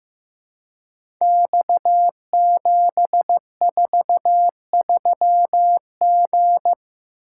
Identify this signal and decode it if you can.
Morse code, heard as a keyed tone, about 15 wpm, 700 Hz. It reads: X743G
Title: Code Groups - 5 Letter:Numbers - 15wpm